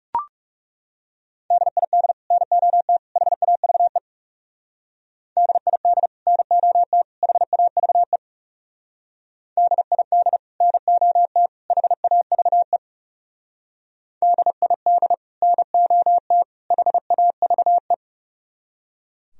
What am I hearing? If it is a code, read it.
Morse code, heard as a keyed tone, about 36 wpm, 700 Hz. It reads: DID NOT HAVE DID NOT HAVE DID NOT HAVE DID NOT HAVE
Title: Speed-Racing - Sets of 3 Words 30wpm